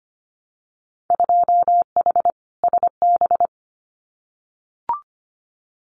Morse code, heard as a keyed tone, about 25 wpm, 700 Hz. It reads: 2H SB
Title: Winter Field Day Exchanges Encore 25wpm